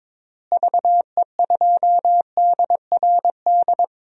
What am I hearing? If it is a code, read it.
Morse code, heard as a keyed tone, about 23 wpm, 700 Hz. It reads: VE2DRD
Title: DXCC-100 Call Signs - 6 Characters - Repeated 3x - 22wpm